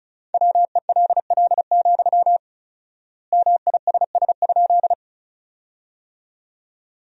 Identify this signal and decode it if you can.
Morse code, heard as a keyed tone, about 36 wpm, 700 Hz. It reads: WELL, MISS?
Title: 4x Word-Spacing - Sentences from Top 500 Words - 35wpm